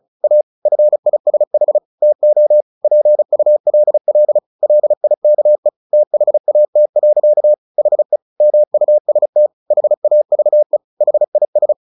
Morse code, heard as a keyed tone, about 34 wpm, 600 Hz. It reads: A FISH TO PULL LIKE THAT. HE MUST HAVE HIS
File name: the old man and the sea / the old man and the sea 1 35wpm